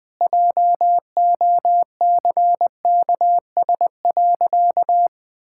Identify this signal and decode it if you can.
Morse code, heard as a keyed tone, about 20 wpm, 700 Hz. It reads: JOCKS.
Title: Pangram Challenge v2 - 20wpm